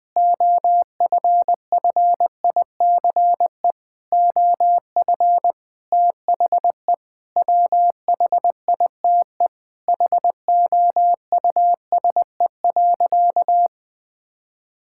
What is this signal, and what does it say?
Morse code, heard as a keyed tone, about 20 wpm, 700 Hz. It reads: OFFICE OF THE WHITE HOUSE.
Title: Sentences with 8-Word Phrases - 20wpm